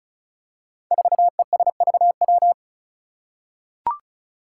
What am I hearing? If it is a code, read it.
Morse code, heard as a keyed tone, about 36 wpm, 700 Hz. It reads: 4ESVW
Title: Five Challenging Letter-Number Combinations 35wpm